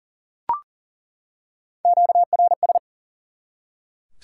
Morse code, heard as a keyed tone, about 40 wpm, 700 Hz. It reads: QRS
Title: Top 90 QSO Elements 40wpm